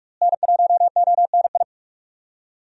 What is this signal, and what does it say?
Morse code, heard as a keyed tone, about 42 wpm, 700 Hz. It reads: N1ONI
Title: USA Call Signs - 5 Characters - Repeated 3x - 45wpm